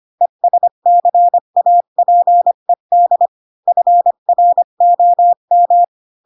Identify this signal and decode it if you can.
Morse code, heard as a keyed tone, about 25 wpm, 700 Hz. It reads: ESCAPED FROM
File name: Pinocchio / Pinocchio ch5 20 25 ik1hge